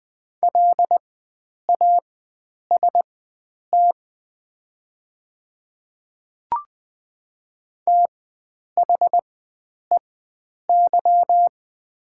Morse code, heard as a keyed tone, about 20 wpm, spaced as for 9 wpm, 700 Hz. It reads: LAST THEY
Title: 4x Character Spacing - Top 500 Words - 20wpm